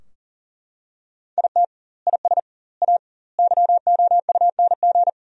Morse code, heard as a keyed tone, about 40 wpm, 700 Hz. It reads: IT IS A YOUNG
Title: 2x Word-Spacing - Sentences from Top 400 Words - 40wpm